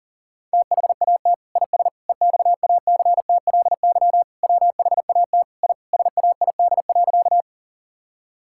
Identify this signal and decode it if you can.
Morse code, heard as a keyed tone, about 40 wpm, 700 Hz. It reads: THAT IS EXACTLY WHAT I SAID.
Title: Sentences from Top 1000 Words - Review Part 6 - 40wpm